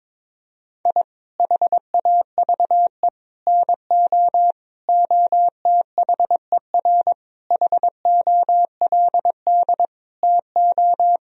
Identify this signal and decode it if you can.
Morse code, heard as a keyed tone, about 22 wpm, 700 Hz. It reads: I HAVE NO OTHER HOLD TO
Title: Sentences from Top 1000 Words - Review Part 2 - 22wpm